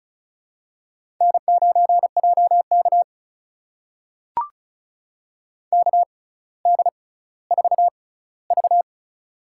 Morse code, heard as a keyed tone, about 36 wpm, spaced as for 18 wpm, 700 Hz. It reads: N9JK KD4V
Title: USA Call Signs - 4 Characters - 6x Character Spacing - 35wpm